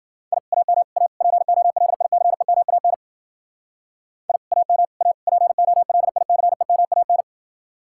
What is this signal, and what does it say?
Morse code, heard as a keyed tone, about 52 wpm, 700 Hz. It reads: I AM A POLICEMAN I AM A POLICEMAN